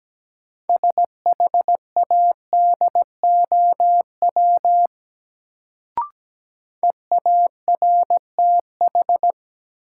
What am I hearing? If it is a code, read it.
Morse code, heard as a keyed tone, about 17 wpm, 700 Hz. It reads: SHADOW EARTH
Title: Sets of 5 Words Encore - New Word Drill - 17wpm